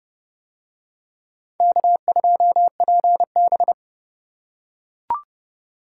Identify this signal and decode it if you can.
Morse code, heard as a keyed tone, about 26 wpm, 700 Hz. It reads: K2PB